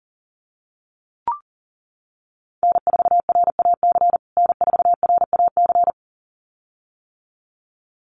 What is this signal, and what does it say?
Morse code, heard as a keyed tone, about 40 wpm, 700 Hz. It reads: N4RAC N4RAC
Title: USA Call Signs Repeated 2x 40wpm